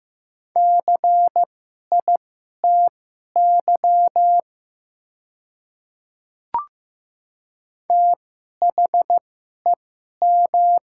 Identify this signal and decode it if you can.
Morse code, heard as a keyed tone, about 15 wpm, spaced as for 11 wpm, 700 Hz. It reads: CITY THEM
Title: 2x Character Spacing - Top 500 Words - 15wpm